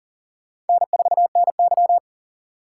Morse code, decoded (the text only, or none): N4NY